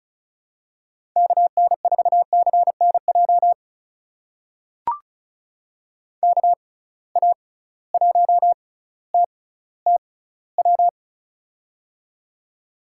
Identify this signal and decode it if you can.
Morse code, heard as a keyed tone, about 36 wpm, spaced as for 18 wpm, 700 Hz. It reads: KN4CNJ KA1TTW
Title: USA Call Signs - 6 Characters - 6x Character Spacing - 35wpm